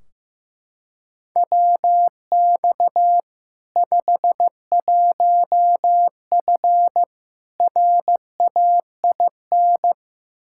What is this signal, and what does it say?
Morse code, heard as a keyed tone, about 15 wpm, 700 Hz. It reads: WX 51F RAIN
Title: K1USN SST QSO Practice 15wpm